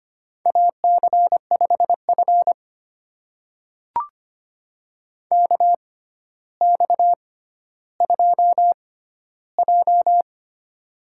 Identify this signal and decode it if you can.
Morse code, heard as a keyed tone, about 25 wpm, spaced as for 13 wpm, 700 Hz. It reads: AC5F KX2J
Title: USA Call Signs - 4 Characters - 6x Character Spacing - 25wpm